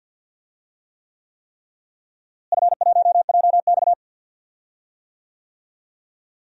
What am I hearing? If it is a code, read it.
Morse code, heard as a keyed tone, about 50 wpm, 700 Hz. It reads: F1JX
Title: DXCC-100 Call Signs - 4 Characters - Repeated 3x - 50wpm